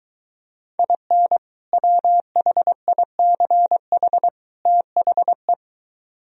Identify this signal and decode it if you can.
Morse code, heard as a keyed tone, about 23 wpm, 700 Hz. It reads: IN WHICH THE